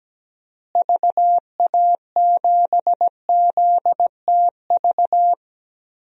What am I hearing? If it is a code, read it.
Morse code, heard as a keyed tone, about 17 wpm, 700 Hz. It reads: VA7ZTV
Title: DXCC-100 Call Signs - 6 Characters - Repeated 2x - 17wpm